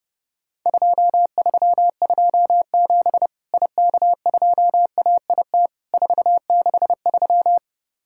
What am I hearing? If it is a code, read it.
Morse code, heard as a keyed tone, about 30 wpm, 700 Hz. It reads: 2327 IK2AIT 463
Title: Worked All Europe DX Contest - QTC Exchange - 30wpm